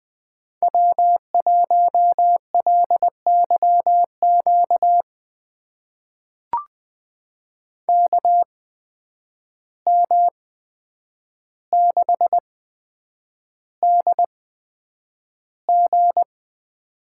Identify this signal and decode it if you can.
Morse code, heard as a keyed tone, about 20 wpm, spaced as for 9 wpm, 700 Hz. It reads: W1LYQ KM6DG
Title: USA Call Signs - 5 Characters - 8x Character Spacing - 20wpm